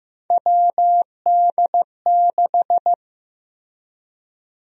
Morse code, heard as a keyed tone, about 15 wpm, 700 Hz. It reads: WD6
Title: USA Call Sign Prefixes - Repeated 2x - 15wpm